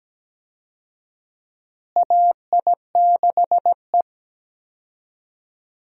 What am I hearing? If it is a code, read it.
Morse code, heard as a keyed tone, about 17 wpm, 700 Hz. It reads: AI6E
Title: USA Call Signs - 4 Characters - Repeated 3x - 17wpm